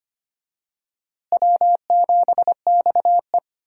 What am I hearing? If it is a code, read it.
Morse code, heard as a keyed tone, about 25 wpm, 700 Hz. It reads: W7XE
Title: USA Call Signs - 4 Characters - Repeated 3x - 25wpm